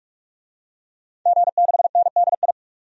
Morse code, heard as a keyed tone, about 46 wpm, 700 Hz. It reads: G6NDI
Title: DXCC-100 Call Signs - 5 Characters - 45wpm